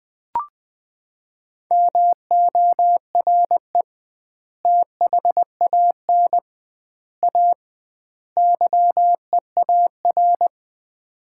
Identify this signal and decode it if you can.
Morse code, heard as a keyed tone, about 20 wpm, 700 Hz. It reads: MORE THAN A YEAR
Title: 2x Word-Spacing - Sets of 4 Words - 20wpm